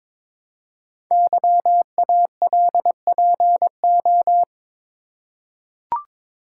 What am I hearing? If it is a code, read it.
Morse code, heard as a keyed tone, about 22 wpm, 700 Hz. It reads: YALPO
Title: Code Groups - 5 Letters - 22wpm